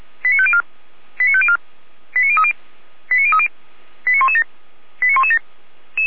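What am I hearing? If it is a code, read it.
Five-tone selective call (ZVEI), heard as quick runs of five tones, tones of about 70 ms. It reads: 89504 89504 89030 89030 89117 89117
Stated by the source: Five-tone ZVEI selective call, received on an SDR.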